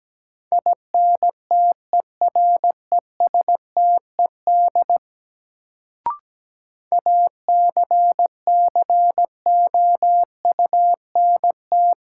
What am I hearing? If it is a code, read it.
Morse code, heard as a keyed tone, about 17 wpm, 700 Hz. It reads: INTERESTED ACCOUNT
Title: Sentences from Top 900 Words - New Word Drill - 17wpm